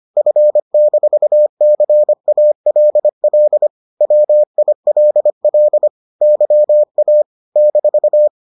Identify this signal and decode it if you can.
Morse code, heard as a keyed tone, about 25 wpm, 600 Hz. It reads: F-CALL WILL YA -